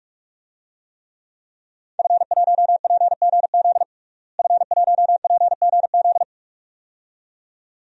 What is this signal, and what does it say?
Morse code, heard as a keyed tone, about 46 wpm, 700 Hz. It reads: F1PGZ F1PGZ
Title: DXCC-100 Call Signs - 5 Characters - Repeated 2x - 45wpm